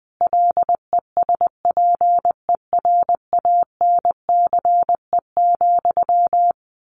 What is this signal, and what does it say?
Morse code, heard as a keyed tone, about 20 wpm, 700 Hz. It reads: LESPERANCE,
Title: A Sound Of Thunder 20wpm